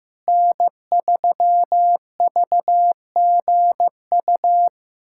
Morse code, heard as a keyed tone, about 15 wpm, 700 Hz. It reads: N3VGU
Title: USA Callsign DE Callsign - 5 Characters - 15wpm